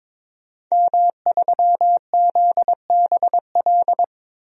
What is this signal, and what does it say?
Morse code, heard as a keyed tone, about 22 wpm, 700 Hz. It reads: M3ZBL
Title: DXCC-100 Call Signs - 5 Characters - Repeated 2x - 22wpm